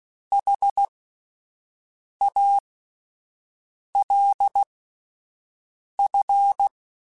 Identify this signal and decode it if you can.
Morse code, heard as a keyed tone, about 16 wpm, spaced as for 5 wpm, 800 Hz. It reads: HALF
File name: Spaced / Chuckles Spaced-mp3s / HalfFull sp @10